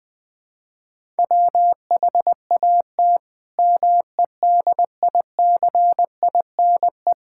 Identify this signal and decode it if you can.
Morse code, heard as a keyed tone, about 20 wpm, 700 Hz. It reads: WHAT MEDICINE
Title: Sentences from Top 1000 Words - Review Part 6 - 20wpm